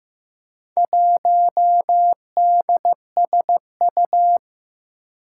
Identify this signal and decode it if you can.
Morse code, heard as a keyed tone, about 15 wpm, 700 Hz. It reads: 1DSU
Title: Four Challenging Letter-Number Combinations 15wpm